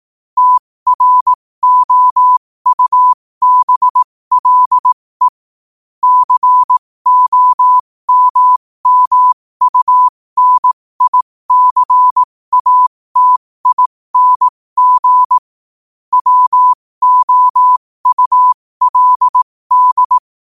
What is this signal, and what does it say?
Morse code, heard as a keyed tone, about 18 wpm, spaced as for 15 wpm, 1000 Hz. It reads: TROUBLE COMMUNICATING WOULD
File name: qotdfiles / 15wpm